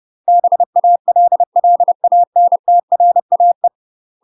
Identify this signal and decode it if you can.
Morse code, heard as a keyed tone, about 30 wpm, 700 Hz. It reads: BALLANTRAE
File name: UK MASTER QSOs TRAINING / 30WPM / UK MASTER QSOs 0011